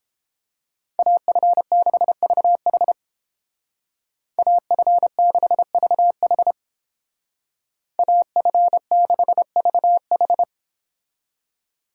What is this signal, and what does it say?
Morse code, heard as a keyed tone, about 29 wpm, 700 Hz. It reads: AF6VH AF6VH AF6VH